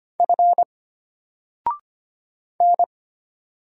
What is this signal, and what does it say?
Morse code, heard as a keyed tone, about 25 wpm, 700 Hz. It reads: F N
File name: Single-Letter-Number-25wpm